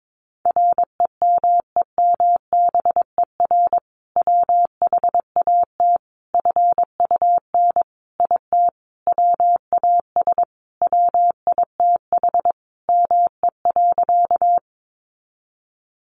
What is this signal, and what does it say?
Morse code, heard as a keyed tone, about 22 wpm, 700 Hz. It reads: REMEMBER WHAT FUN IT WAS WITH ME.
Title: Sentences from Top 700 Words 22wpm